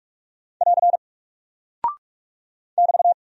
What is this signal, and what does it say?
Morse code, heard as a keyed tone, about 44 wpm, 700 Hz. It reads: <AR> <BT>
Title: Single Character (Letter:Number:Punctuation:Prosign) 45wpm